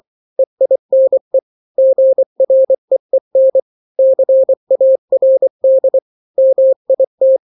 Morse code, heard as a keyed tone, about 24 wpm, 530 Hz. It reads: EINE GREEN CARD MIT
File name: daily / 23wpm